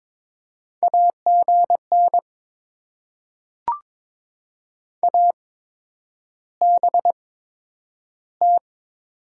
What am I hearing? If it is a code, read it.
Morse code, heard as a keyed tone, about 22 wpm, spaced as for 9 wpm, 700 Hz. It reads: AGN ABT
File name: Top 90 QSO Elements - 8x Character Spacing - 22wpm